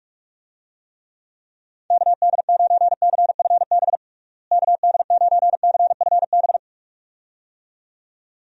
Morse code, heard as a keyed tone, about 46 wpm, 700 Hz. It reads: KD9CFB KD9CFB
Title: USA Call Signs - 6 Characters - Repeated 2x - 45wpm